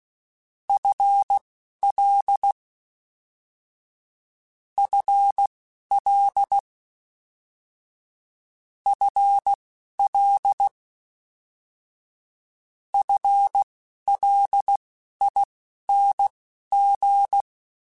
Morse code, heard as a keyed tone, about 16 wpm, spaced as for 11 wpm, 800 Hz. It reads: FL FL FL FLING